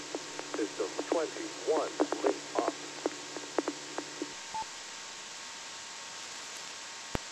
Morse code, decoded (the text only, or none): none